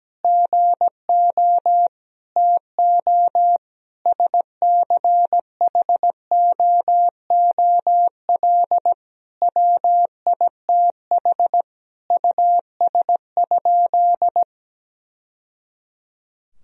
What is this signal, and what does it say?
Morse code, heard as a keyed tone, about 17 wpm, 700 Hz. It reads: GO TO SCHOOL WITH US?
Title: Sentences from Top 1000 Words - Review Part 3 - 17wpm